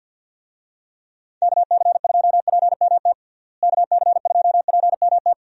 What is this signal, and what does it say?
Morse code, heard as a keyed tone, about 50 wpm, 700 Hz. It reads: KC2PMT KC2PMT
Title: USA Call Signs - 6 Characters - Repeated 2x - 50wpm